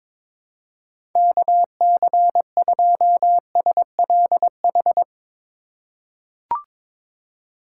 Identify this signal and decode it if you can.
Morse code, heard as a keyed tone, about 22 wpm, 700 Hz. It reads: KC2SLH
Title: USA Call Signs - 6 Characters - 3x Character Spacing - 22wpm